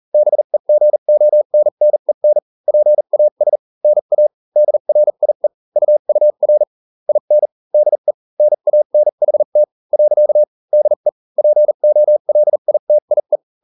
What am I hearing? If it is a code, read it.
Morse code, heard as a keyed tone, about 40 wpm, spaced as for 35 wpm, 600 Hz. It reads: BEGONNEN PAS NA DRIE UUR IN DE NANHT . DE POLITIE